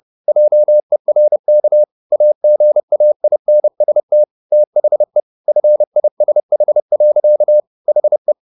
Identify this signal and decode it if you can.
Morse code, heard as a keyed tone, about 30 wpm, 600 Hz. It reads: JERK AGAINST THE FISH. HE